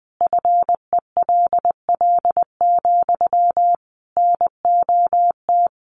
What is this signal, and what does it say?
Morse code, heard as a keyed tone, about 20 wpm, 700 Hz. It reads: FELL, NOT
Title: A Sound Of Thunder 20wpm